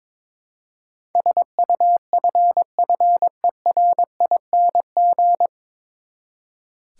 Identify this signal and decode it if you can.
Morse code, heard as a keyed tone, about 22 wpm, 700 Hz. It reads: SUFFERING
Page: Morse Code Ninja practice 22wpm